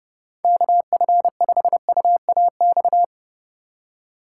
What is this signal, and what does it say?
Morse code, heard as a keyed tone, about 31 wpm, 700 Hz. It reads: KF5UAX